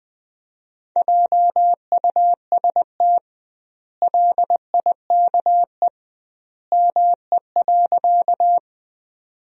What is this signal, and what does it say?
Morse code, heard as a keyed tone, about 21 wpm, 700 Hz. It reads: JUST LIKE ME.
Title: 2x Word-Spacing - Sentences from Top 200 Words - 20wpm